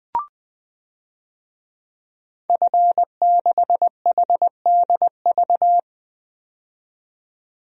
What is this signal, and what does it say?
Morse code, heard as a keyed tone, about 21 wpm, 700 Hz. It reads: F6HDV
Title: DXCC-100 Call Signs - 5 Characters - 20wpm